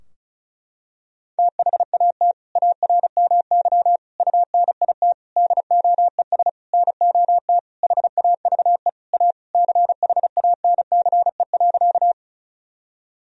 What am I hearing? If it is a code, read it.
Morse code, heard as a keyed tone, about 36 wpm, 700 Hz. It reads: THAT ARMY UNIT DOES NOT HAVE A CHANCE.
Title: Sentences from Top 1000 Words - Review Part 2 - 35wpm